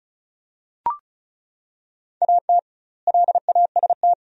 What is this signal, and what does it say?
Morse code, heard as a keyed tone, about 34 wpm, 700 Hz. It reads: AT LAST
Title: 2x Word-Spacing - Sentences from Top 300 Words - 35wpm